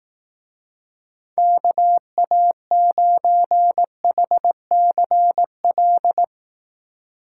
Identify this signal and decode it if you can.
Morse code, heard as a keyed tone, about 18 wpm, 700 Hz. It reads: KA9HCL
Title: Speed Racing - USA 2x3 Call Signs 15wpm